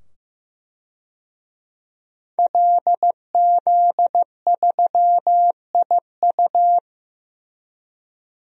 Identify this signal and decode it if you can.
Morse code, heard as a keyed tone, about 15 wpm, 700 Hz. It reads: LZ3IU
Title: DXCC-100 Call Signs - 5 Characters - 15wpm